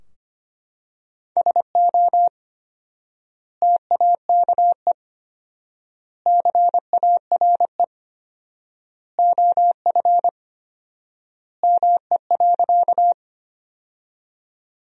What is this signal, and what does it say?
Morse code, heard as a keyed tone, about 25 wpm, 700 Hz. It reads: SO TAKE CARE OF ME.
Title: 4x Word-Spacing - Sentences from Top 300 Words - 25wpm